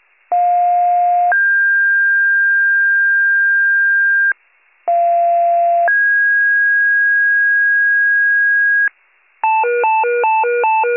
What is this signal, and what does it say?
Two-tone page, heard as a one-second tone, then a three-second tone, about 710 then 1600 Hz, about 690 then 1650 Hz. It is heard from 0.3 to 8.9 s.